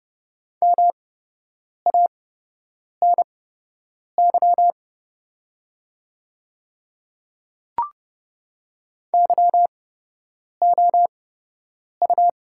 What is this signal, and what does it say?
Morse code, heard as a keyed tone, about 30 wpm, spaced as for 8 wpm, 700 Hz. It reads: MANY YOU
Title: 8x Character Spacing - Top 300 Words - 30wpm